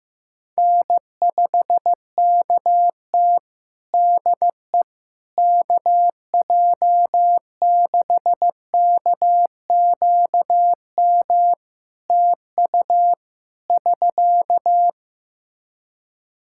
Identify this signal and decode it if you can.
Morse code, heard as a keyed tone, about 15 wpm, 700 Hz. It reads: N5KT DE KJ6KQM TU <SK>
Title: Practice QSO - Exchange 4 - Ending - 15wpm